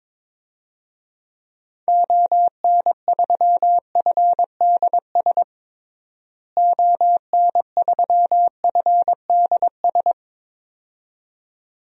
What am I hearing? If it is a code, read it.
Morse code, heard as a keyed tone, about 21 wpm, 700 Hz. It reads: ON3FDS ON3FDS
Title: DXCC-100 Call Signs - 6 Characters - Repeated 2x - 22wpm